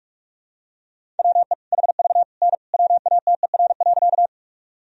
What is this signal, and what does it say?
Morse code, heard as a keyed tone, about 46 wpm, 700 Hz. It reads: WE HV N WATER.
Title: Sentences from Top 100 Words and Abbreviations 45wpm